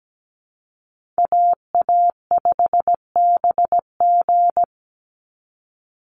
Morse code, heard as a keyed tone, about 17 wpm, 700 Hz. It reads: AA5BG
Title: Call Signs - 17wpm